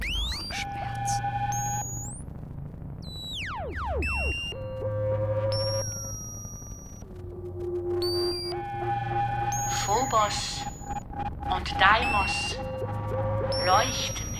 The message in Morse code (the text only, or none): none